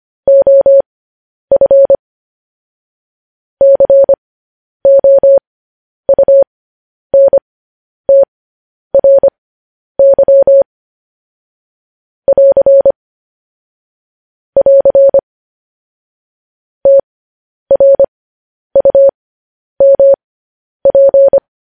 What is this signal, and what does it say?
Morse code, heard as a keyed tone, about 25 wpm, spaced as for 10 wpm, 550 Hz. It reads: OF COUNTRY <AR> <AR> TRUMP